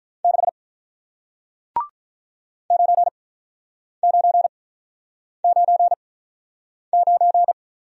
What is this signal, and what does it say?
Morse code, heard as a keyed tone, about 46 wpm, spaced as for 9 wpm, 700 Hz. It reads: 6 9999